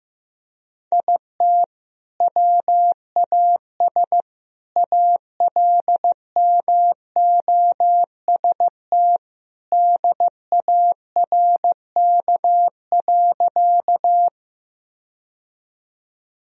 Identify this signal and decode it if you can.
Morse code, heard as a keyed tone, about 15 wpm, 700 Hz. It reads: IT WAS ALMOST DARK.
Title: Sentences from Top 1000 Words - Review Part 4 - 15wpm